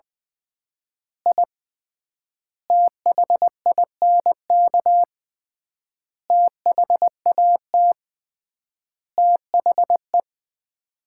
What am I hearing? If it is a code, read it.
Morse code, heard as a keyed tone, about 20 wpm, 700 Hz. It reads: I THINK THAT THE
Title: 3x Word-Spacing - Sets of 5 Words - 20wpm